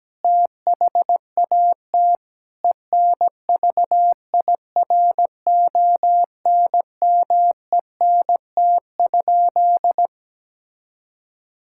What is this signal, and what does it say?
Morse code, heard as a keyed tone, about 17 wpm, 700 Hz. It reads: THAT ENVIRONMENT?
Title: Sentences from Top 1000 Words - Review Part 5 - 17wpm